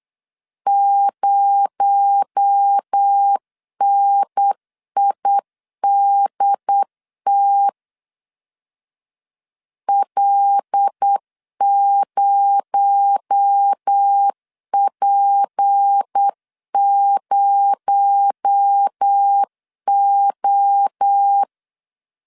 Morse code, heard as a keyed tone, about 8 wpm, 800 Hz. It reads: 0NIDT L0P0O
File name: LEZIONI / 6 PKX0 / 6 PKX0 8wpm 3341